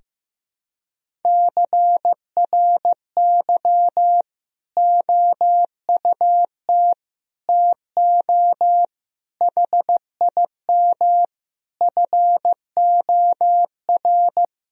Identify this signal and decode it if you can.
Morse code, heard as a keyed tone, about 15 wpm, 700 Hz. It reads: CRY OUT TO HIM FOR
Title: Sentences from Top 1000 Words - Review Part 2 - 15wpm